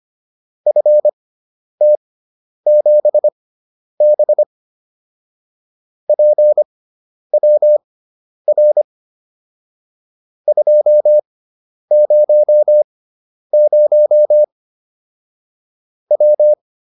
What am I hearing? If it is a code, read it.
Morse code, heard as a keyed tone, about 25 wpm, spaced as for 10 wpm, 600 Hz. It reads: FT7B PWR 200 W
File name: QSO407 10